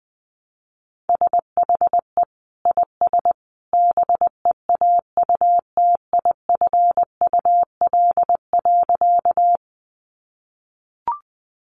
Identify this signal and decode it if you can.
Morse code, heard as a keyed tone, about 20 wpm, 700 Hz. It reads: SHE IS BEAUTIFUL.
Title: Sentences from Top 900 Words 20wpm